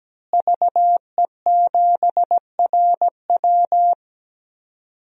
Morse code, heard as a keyed tone, about 17 wpm, 700 Hz. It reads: VE7RW
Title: DXCC-100 Call Signs - 5 Characters - Repeated 3x - 17wpm